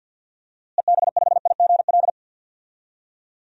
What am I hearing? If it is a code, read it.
Morse code, heard as a keyed tone, about 46 wpm, 700 Hz. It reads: EB5IGL
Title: DXCC-100 Call Signs - 6 Characters - Repeated 3x - 50wpm